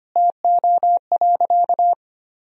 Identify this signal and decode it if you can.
Morse code, heard as a keyed tone, about 25 wpm, 700 Hz. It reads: TO.
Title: 4x Word-Spacing - Sentences from Top 400 Words - 25wpm